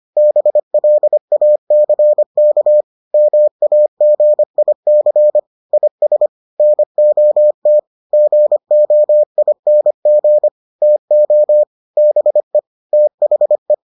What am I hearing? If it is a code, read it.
Morse code, heard as a keyed tone, about 25 wpm, 600 Hz. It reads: BLACK MAGIC IS NOT GOING TO BE THE